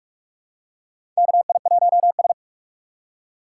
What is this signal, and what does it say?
Morse code, heard as a keyed tone, about 45 wpm, 700 Hz. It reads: KI1S
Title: USA Call Signs - 3x Character Spacing - 45wpm